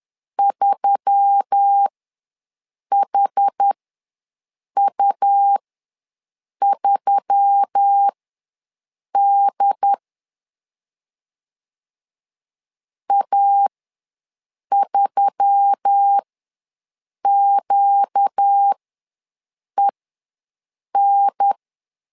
Morse code, heard as a keyed tone, about 10 wpm, spaced as for 6 wpm, 800 Hz. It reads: 3HU3D A3QEN
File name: LEZIONI / 4 Q1S3 ws 10wpm 2437